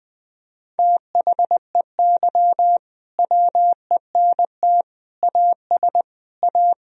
Morse code, heard as a keyed tone, about 21 wpm, 700 Hz. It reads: THEY WENT AS A
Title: Sentences from Top 1000 Words - Review Part 1 - 20wpm